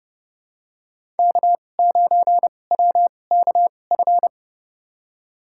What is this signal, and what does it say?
Morse code, heard as a keyed tone, about 30 wpm, spaced as for 22 wpm, 700 Hz. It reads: K9WKF
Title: USA Call Signs - 5 Characters - 2x Character Spacing - 30wpm